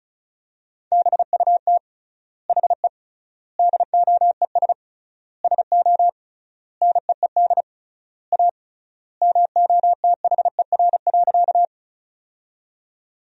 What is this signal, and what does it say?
Morse code, heard as a keyed tone, about 34 wpm, 700 Hz. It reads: BUT HE DOES SO NEED A MOTHER.
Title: 3x Word-Spacing - Sentences from Top 200 Words - 35wpm